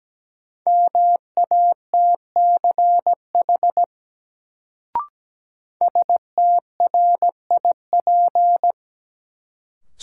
Morse code, heard as a keyed tone, about 17 wpm, 700 Hz. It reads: MATCH STRIP
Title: Sentences from Top 1600 Words - New Word Drill Alt - 17wpm